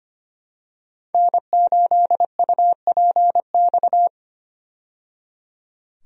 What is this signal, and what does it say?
Morse code, heard as a keyed tone, about 25 wpm, 700 Hz. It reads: N8UPX
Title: USA Call Signs - 5 Characters - 25wpm